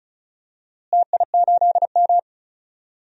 Morse code, heard as a keyed tone, about 34 wpm, 700 Hz. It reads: TI8M
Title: DXCC-100 Call Signs - 4 Characters - Repeated 2x - 35wpm